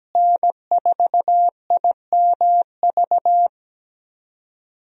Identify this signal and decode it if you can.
Morse code, heard as a keyed tone, about 17 wpm, 700 Hz. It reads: N4IMV
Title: USA Call Signs - 5 Characters - Repeated 2x - 17wpm